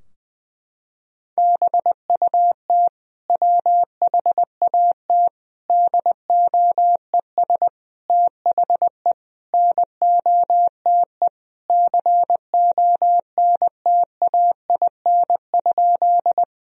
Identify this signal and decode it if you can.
Morse code, heard as a keyed tone, about 20 wpm, 700 Hz. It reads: BUT WHAT DOES THE NOTE CONTAIN?
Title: Sentences from Top 1000 Words - Review Part 2 - 20wpm